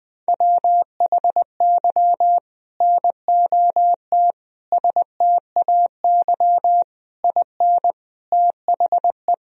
Morse code, heard as a keyed tone, about 20 wpm, 700 Hz. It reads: WHY NOT STAY IN THE
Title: Sentences from Top 1000 Words - Review Part 1 - 20wpm